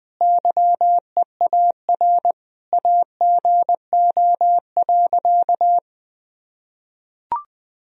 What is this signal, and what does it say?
Morse code, heard as a keyed tone, about 19 wpm, 700 Hz. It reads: YEAR AGO.
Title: Sentences from Top 1000 Words - Review Part 3 - 20wpm